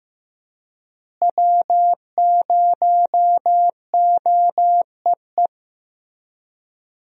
Morse code, heard as a keyed tone, about 16 wpm, 700 Hz. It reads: W0OEE
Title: USA Call Signs - 5 Characters - Repeated 3x - 15wpm